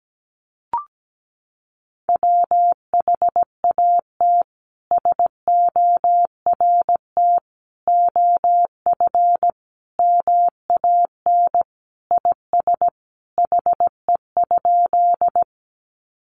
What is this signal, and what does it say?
Morse code, heard as a keyed tone, about 17 wpm, 700 Hz. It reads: WHAT SORT OF MAN IS HE?
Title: Sentences from Top 1300 Words 17wpm